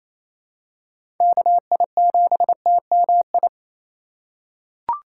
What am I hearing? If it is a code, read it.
Morse code, heard as a keyed tone, about 28 wpm, 700 Hz. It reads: KI7TMI
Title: USA Call Signs - 6 Characters - 3x Character Spacing - 28wpm